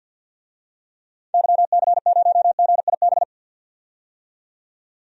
Morse code, heard as a keyed tone, about 46 wpm, 700 Hz. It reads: YC0GIB
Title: DXCC-100 Call Signs - 6 Characters - 50wpm